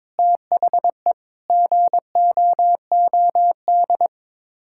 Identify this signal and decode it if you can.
Morse code, heard as a keyed tone, about 21 wpm, 700 Hz. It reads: THE GOOD
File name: Sets of 9 Words - 22wpm